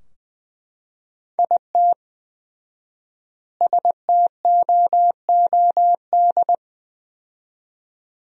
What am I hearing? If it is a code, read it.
Morse code, heard as a keyed tone, about 20 wpm, 700 Hz. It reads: IT STOOD
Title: 4x Word-Spacing - Sentences from Top 500 Words - 20wpm